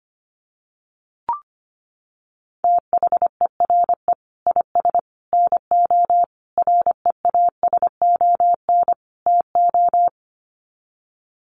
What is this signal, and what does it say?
Morse code, heard as a keyed tone, about 25 wpm, 700 Hz. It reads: THERE IS NO REASON TO
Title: Sets of 5 Words Encore 25wpm